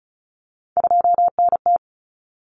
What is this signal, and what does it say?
Morse code, heard as a keyed tone, about 34 wpm, 700 Hz. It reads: 2NT
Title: Common Cut Numbers - 3-Digit 35wpm